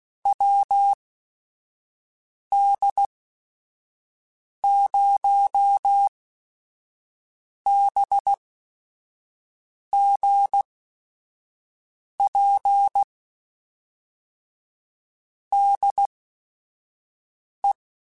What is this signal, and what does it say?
Morse code, heard as a keyed tone, about 16 wpm, spaced as for 5 wpm, 800 Hz. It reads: WD0BGP DE
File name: QSOs / QSOs-mp3s / WD0BGP 1@5